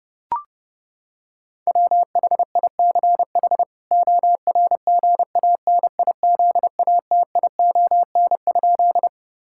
Morse code, heard as a keyed tone, about 32 wpm, 700 Hz. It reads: WHICH ORGANIZATION?
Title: Sentences from Top 1000 Words - Review Part 5 - 30wpm